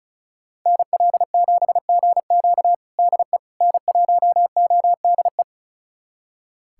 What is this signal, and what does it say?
Morse code, heard as a keyed tone, about 36 wpm, 700 Hz. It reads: NL7GQ DE N1ODE
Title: USA Callsign DE Callsign - 5 Characters - 35wpm